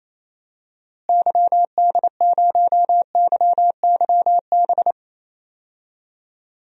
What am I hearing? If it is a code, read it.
Morse code, heard as a keyed tone, about 28 wpm, 700 Hz. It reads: YD0YYB
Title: DXCC-100 Call Signs - 6 Characters - Repeated 3x - 28wpm